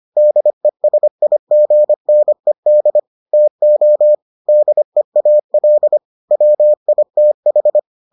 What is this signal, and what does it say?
Morse code, heard as a keyed tone, about 25 wpm, 600 Hz. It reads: DESIGNED TO DEAL WITH